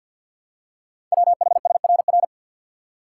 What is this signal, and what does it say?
Morse code, heard as a keyed tone, about 49 wpm, 700 Hz. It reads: WHSRR
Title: Code Groups - 5 Letters - 50wpm